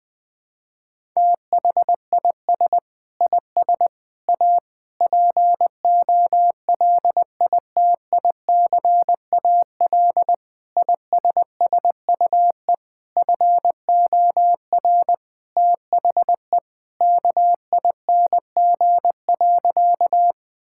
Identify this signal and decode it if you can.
Morse code, heard as a keyed tone, about 20 wpm, 700 Hz. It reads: THIS IS A POLITICAL ISSUE FOR THE KING.
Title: Sentences from Top 1000 Words - Review Part 6 - 20wpm